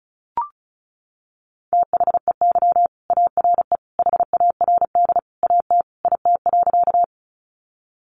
Morse code, heard as a keyed tone, about 34 wpm, 700 Hz. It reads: THEY ARE HARD AT IT.
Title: Sentences from Top 300 Words 35wpm